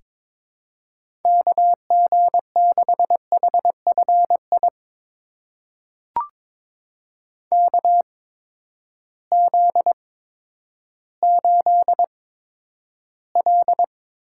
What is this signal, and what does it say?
Morse code, heard as a keyed tone, about 22 wpm, spaced as for 11 wpm, 700 Hz. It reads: KG6HFI KZ8L
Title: USA Call Signs - 8x Character Spacing - 22wpm